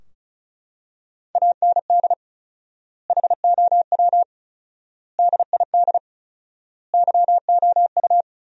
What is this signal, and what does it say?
Morse code, heard as a keyed tone, about 36 wpm, 700 Hz. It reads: AND HOW DID YOU
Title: 4x Word-Spacing - Sentences from Top 100 Words - 35wpm